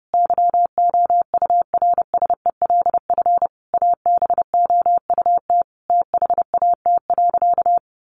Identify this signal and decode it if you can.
Morse code, heard as a keyed tone, about 30 wpm, 700 Hz. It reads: YOURSELF ABOUT THAT.
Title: Sentences from Top 1200 Words 30wpm